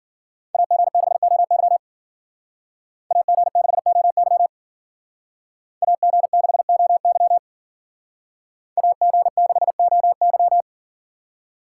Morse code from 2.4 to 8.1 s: AG6OY AG6OY

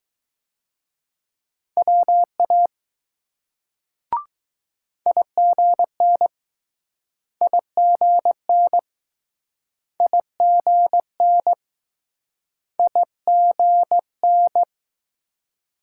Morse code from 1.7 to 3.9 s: WA

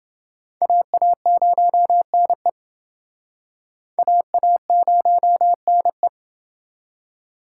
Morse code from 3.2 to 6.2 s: AA0NE

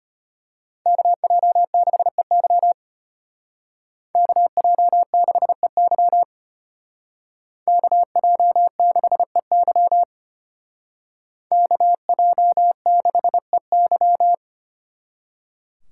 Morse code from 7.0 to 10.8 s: KJ6EY

